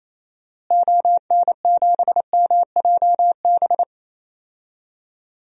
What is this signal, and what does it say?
Morse code, heard as a keyed tone, about 27 wpm, 700 Hz. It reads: ON7MJB
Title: DXCC-100 Call Signs - 6 Characters - Repeated 3x - 28wpm